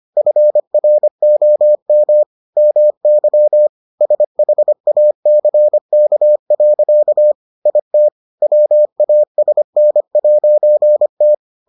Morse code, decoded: FROM MY SHACK. IT WASN'T

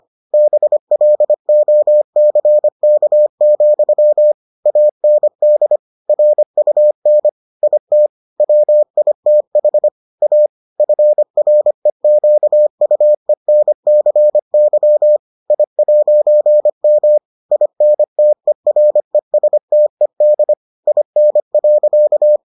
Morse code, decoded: BLOCK, AND RUN IT WITH A FREQUENCY I'M INTERESTED IN.

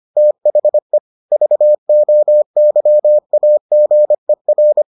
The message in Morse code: THE VOYAGER

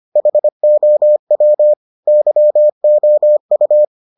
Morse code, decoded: HOW YOU

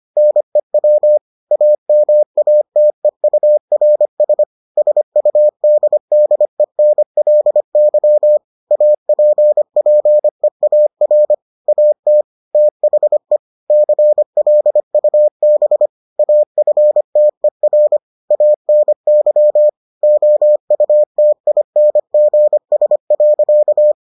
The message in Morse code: NEW AMATEURS SUDDENLY APPEAR AT THE CLUB AFTER ANY OUTINGS.

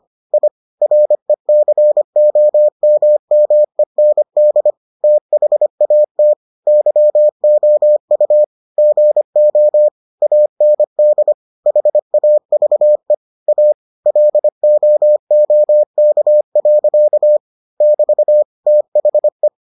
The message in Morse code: I RECOMMEND THAT YOU GO AND HAVE A LOOK. <BT> THE

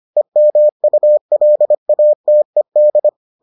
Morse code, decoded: EMULATED

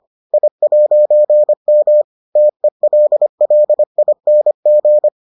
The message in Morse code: I'M TELLING

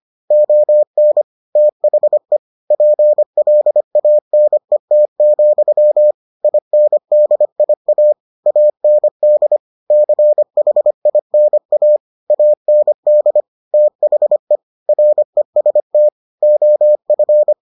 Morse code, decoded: ON THE PLANET, INDIA AND CHINA AND THE REST OF

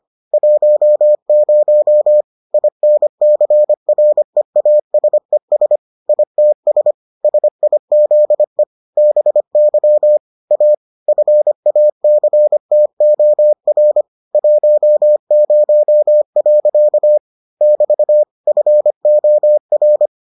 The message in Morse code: 10 INCREASES ITS SIZE BY A FACTOR 10. <BT> FOR

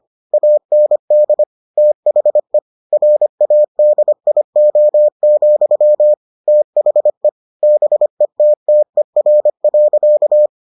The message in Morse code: AND THE RADIO, THE BETTER.

